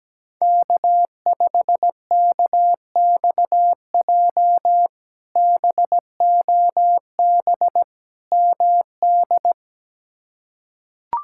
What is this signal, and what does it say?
Morse code, heard as a keyed tone, about 17 wpm, 700 Hz. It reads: K5KXJ BOB MD
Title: K1USN SST QSO Practice - 17wpm